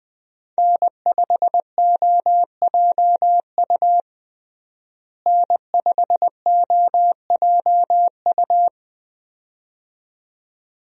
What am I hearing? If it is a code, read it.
Morse code, heard as a keyed tone, about 20 wpm, 700 Hz. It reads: N5OJU N5OJU
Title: USA Call Signs - 5 Characters - Repeated 2x - 20wpm